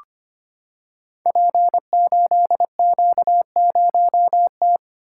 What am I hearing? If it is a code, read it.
Morse code, heard as a keyed tone, about 25 wpm, 700 Hz. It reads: P8Q0T
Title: Code Groups - 5 Letter:Numbers - 25wpm